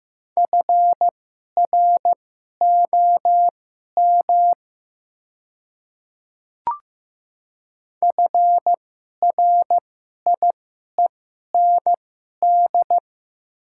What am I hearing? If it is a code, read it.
Morse code, heard as a keyed tone, about 15 wpm, spaced as for 11 wpm, 690 Hz. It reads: FROM FRIEND
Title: 2x Character Spacing - Top 300 Words - 15wpm